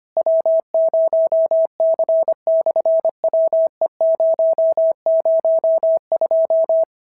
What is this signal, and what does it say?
Morse code, heard as a keyed tone, about 25 wpm, 650 Hz. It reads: W0C/WE002
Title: SOTA QSO Practice 25wpm